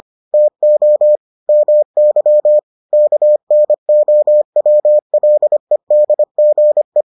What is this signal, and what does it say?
Morse code, heard as a keyed tone, about 25 wpm, 600 Hz. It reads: TO MY KNOWLEDGE